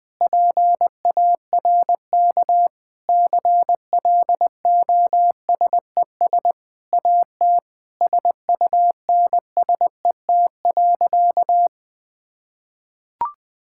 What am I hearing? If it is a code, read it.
Morse code, heard as a keyed tone, about 21 wpm, 700 Hz. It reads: PARK CLOSES AT SUNSET.
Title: Sentences with 7-Word Phrases - 20wpm